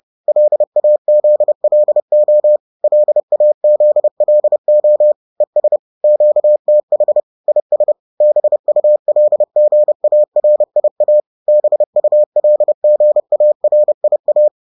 Morse code, heard as a keyed tone, about 30 wpm, 600 Hz. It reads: LAZLO LAZLO ES QTH IS BULGARIA BULGARIA